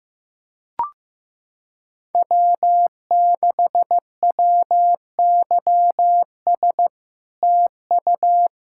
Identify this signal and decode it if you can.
Morse code, heard as a keyed tone, about 16 wpm, 700 Hz. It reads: W6WYS TU
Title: SOTA QSO Practice 15wpm